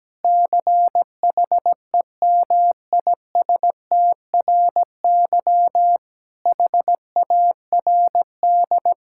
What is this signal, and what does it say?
Morse code, heard as a keyed tone, about 17 wpm, 700 Hz. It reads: CHEMISTRY HARD
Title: Sentences from Top 1000 Words - Review Part 6 - 17wpm